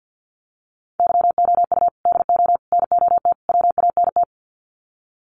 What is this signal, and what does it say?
Morse code, heard as a keyed tone, about 50 wpm, 700 Hz. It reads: YOU DO NOT WANT
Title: Sets of 4 Words 50wpm